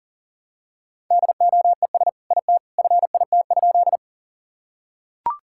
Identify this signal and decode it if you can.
Morse code, heard as a keyed tone, about 40 wpm, 700 Hz. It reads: DOES IT FIT?
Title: Sentences from Top 1000 Words - Review Part 5 - 40wpm